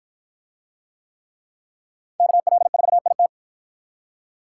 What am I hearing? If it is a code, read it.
Morse code, heard as a keyed tone, about 51 wpm, 700 Hz. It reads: KL4IT